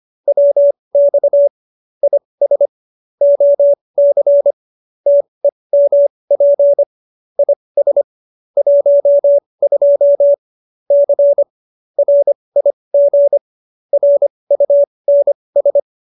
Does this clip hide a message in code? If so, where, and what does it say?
Yes: Morse code from 0.3 to 15.8 s, heard as a keyed tone, about 25 wpm, spaced as for 20 wpm, 570 Hz. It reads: WX IS OC TEMP IS 12 C RIG RUNS